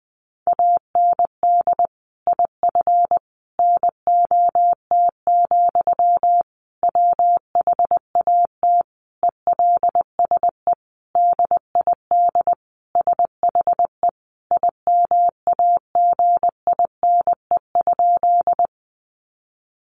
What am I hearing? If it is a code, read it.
Morse code, heard as a keyed tone, about 20 wpm, 700 Hz. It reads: AND IF NOT, WHAT ELSE DID SHE IMAGINE?
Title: Sentences from Top 1400 Words 20wpm